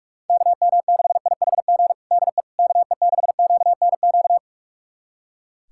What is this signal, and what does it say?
Morse code, heard as a keyed tone, about 45 wpm, 700 Hz. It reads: KM6IHG DE KE6QNQ
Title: USA Callsign DE Callsign - 6 Characters - 45wpm